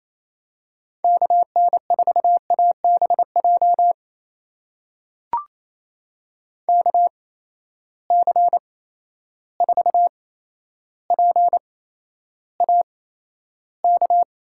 Morse code, heard as a keyed tone, about 28 wpm, spaced as for 12 wpm, 700 Hz. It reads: KN4ABJ KC4PAK
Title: USA Call Signs - 6 Characters - 8x Character Spacing - 28wpm